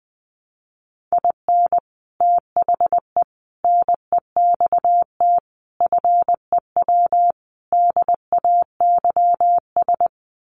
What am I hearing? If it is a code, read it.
Morse code, heard as a keyed tone, about 19 wpm, 700 Hz. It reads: IN THE NEXT FEW DAYS
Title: Sets of 5 Words Encore2 20wpm